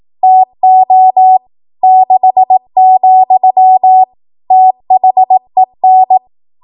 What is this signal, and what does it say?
Morse code, heard as a keyed tone, about 18 wpm, 750 Hz. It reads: TO 6, THEN
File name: Archive / 18 WPM / 18WPM